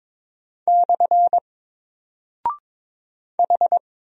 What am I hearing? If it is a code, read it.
Morse code, heard as a keyed tone, about 22 wpm, 700 Hz. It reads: / H
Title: Single Character (Letter:Number:Punctuation:Prosign) 22wpm